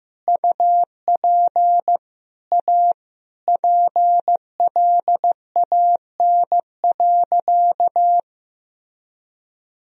Morse code, heard as a keyed tone, about 15 wpm, 700 Hz. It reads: UP A PLAN.
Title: Sentences from Top 1000 Words - Review Part 2 - 15wpm